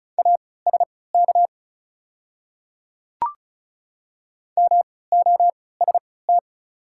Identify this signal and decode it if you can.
Morse code, heard as a keyed tone, about 36 wpm, spaced as for 20 wpm, 700 Hz. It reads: ASK MOST
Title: 3x Character Spacing - Top 200 Words - 35wpm